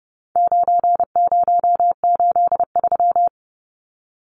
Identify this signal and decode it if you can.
Morse code, heard as a keyed tone, about 30 wpm, 700 Hz. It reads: 9083
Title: Numbers - 4-Digit 30wpm